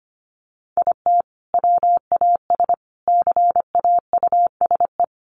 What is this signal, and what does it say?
Morse code, heard as a keyed tone, about 25 wpm, 700 Hz. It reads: IT WAS CAUSE
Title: Sentences from Top 1000 Words 25wpm